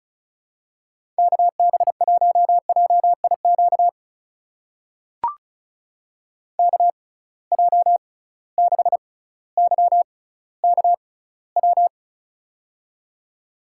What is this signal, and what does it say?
Morse code, heard as a keyed tone, about 34 wpm, spaced as for 18 wpm, 700 Hz. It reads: KB1JIQ KJ6YKW
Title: USA Call Signs - 6 Characters - 6x Character Spacing - 35wpm